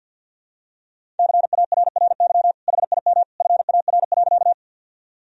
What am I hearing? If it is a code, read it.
Morse code, heard as a keyed tone, about 52 wpm, 700 Hz. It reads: CARRY HIM FAR.
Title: Sentences from Top 1000 Words - Review Part 1 - 50wpm